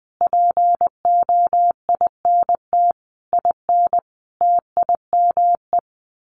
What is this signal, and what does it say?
Morse code, heard as a keyed tone, about 20 wpm, 700 Hz. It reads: POINT IN TIME
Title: Sets of 5 Words Encore 20wpm